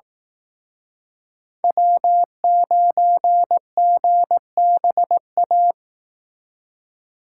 Morse code, heard as a keyed tone, about 18 wpm, 700 Hz. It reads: W9GBA
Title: Speed Racing - USA 1x3 Call Signs 15wpm